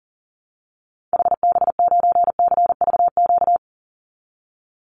Morse code, heard as a keyed tone, about 40 wpm, 700 Hz. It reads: HB9CVQ
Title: Call Signs Repeated 2x - 40wpm